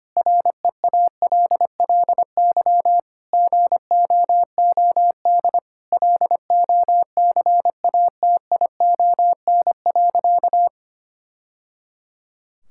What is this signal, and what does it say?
Morse code, heard as a keyed tone, about 25 wpm, 700 Hz. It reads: REALLY GOOD LOCATION.
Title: Sentences from Top 1000 Words - Review Part 6 - 25wpm